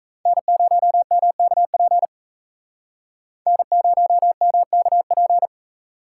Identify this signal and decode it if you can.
Morse code, heard as a keyed tone, about 35 wpm, 700 Hz. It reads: N0MKP N0MKP